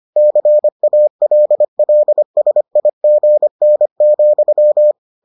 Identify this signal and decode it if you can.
Morse code, heard as a keyed tone, about 25 wpm, 600 Hz. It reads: CALLSIGN,